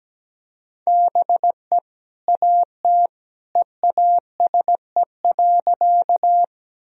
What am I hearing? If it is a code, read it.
Morse code, heard as a keyed tone, about 17 wpm, 700 Hz. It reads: BE AT EASE.
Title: Sentences from Top 1000 Words - Review Part 1 - 17wpm